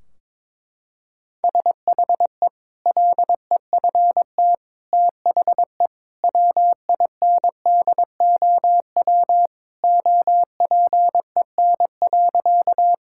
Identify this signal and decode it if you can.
Morse code, heard as a keyed tone, about 22 wpm, 700 Hz. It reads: SHE LEFT THE WINDOW OPEN.
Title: Sentences from Top 1600 Words 22wpm